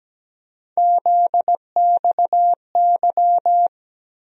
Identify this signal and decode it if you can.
Morse code, heard as a keyed tone, about 17 wpm, 700 Hz. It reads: ZXY
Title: USA Call Sign Postfixes - 3 Letters 17wpm